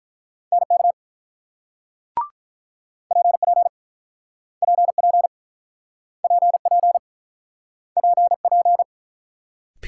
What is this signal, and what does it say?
Morse code, heard as a keyed tone, about 48 wpm, 700 Hz. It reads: NK PP PP PP PP